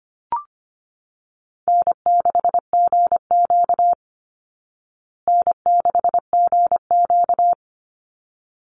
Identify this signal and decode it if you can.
Morse code, heard as a keyed tone, about 25 wpm, 700 Hz. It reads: N6GQ N6GQ
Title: Easy Call Signs Repeated 2x - 25wpm